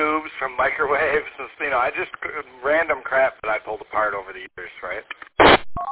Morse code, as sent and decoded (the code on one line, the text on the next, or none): none